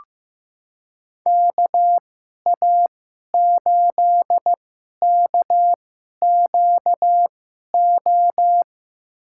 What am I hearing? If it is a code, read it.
Morse code, heard as a keyed tone, about 15 wpm, spaced as for 11 wpm, 700 Hz. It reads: KA8KQO
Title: USA Call Signs - 6 Characters - 2x Character Spacing - 15wpm